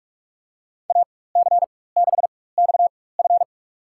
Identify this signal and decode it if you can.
Morse code, heard as a keyed tone, about 48 wpm, spaced as for 21 wpm, 700 Hz. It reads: AC6XF
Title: USA Call Signs - 4x Character Spacing - 45wpm